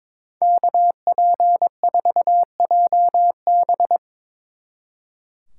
Morse code, heard as a keyed tone, about 23 wpm, 700 Hz. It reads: KP4JB
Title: Speed Racing - USA 2x2 Call Signs 22wpm